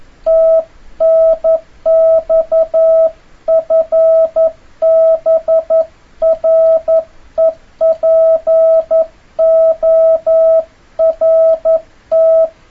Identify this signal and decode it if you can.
Morse code, heard as a keyed tone, about 11 wpm, 650 Hz. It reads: TNX FB REPORT
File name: jh8cbh / 2026cw